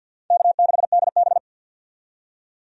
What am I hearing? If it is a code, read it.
Morse code, heard as a keyed tone, about 52 wpm, 700 Hz. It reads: K6DB
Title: USA Call Signs - 4 Characters - 50wpm